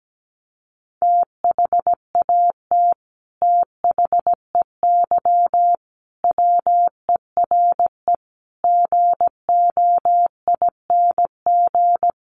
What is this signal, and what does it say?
Morse code, heard as a keyed tone, about 17 wpm, 700 Hz. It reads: THAT THEY WERE GOING
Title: Sets of 5 Words Encore 17wpm